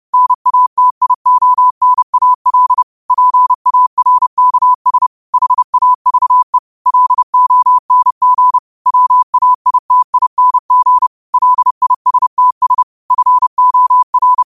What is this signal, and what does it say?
Morse code, heard as a keyed tone, about 30 wpm, 1000 Hz. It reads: NATIONAL PARKS HAVE LONG WAITING LISTS FOR
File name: qotdfiles / 30wpm